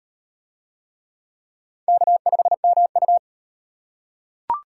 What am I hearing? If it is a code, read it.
Morse code, heard as a keyed tone, about 38 wpm, 700 Hz. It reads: K5MU